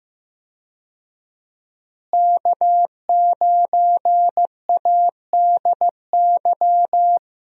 Morse code, heard as a keyed tone, about 16 wpm, 700 Hz. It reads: K9ADY
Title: USA Call Signs - 5 Characters - Repeated 3x - 15wpm